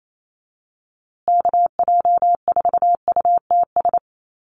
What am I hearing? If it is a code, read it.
Morse code, heard as a keyed tone, about 28 wpm, 700 Hz. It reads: KJ4UTS
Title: USA Call Signs Repeated 3x 28wpm